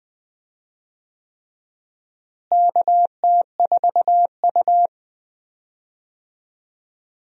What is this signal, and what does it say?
Morse code, heard as a keyed tone, about 20 wpm, 700 Hz. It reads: KT4U
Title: USA Call Signs - 4 Characters - Repeated 3x - 20wpm